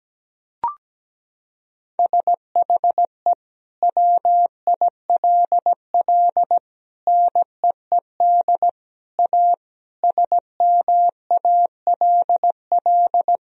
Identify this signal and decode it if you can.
Morse code, heard as a keyed tone, about 17 wpm, 700 Hz. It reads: SHE WILL NEED A SMALL
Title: Sentences from Top 1000 Words - Review Part 2 - 17wpm